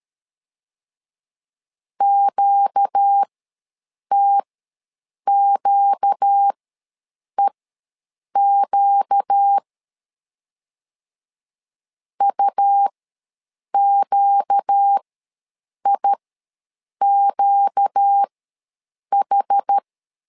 Morse code, heard as a keyed tone, about 13 wpm, spaced as for 7 wpm, 800 Hz. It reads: QTQEQ UQIQH
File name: LEZIONI / 4 Q ws 12wpm 1912